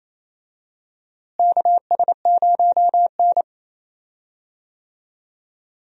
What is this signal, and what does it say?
Morse code, heard as a keyed tone, about 28 wpm, 700 Hz. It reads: KS0N
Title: USA Call Signs - 4 Characters - Repeated 2x - 28wpm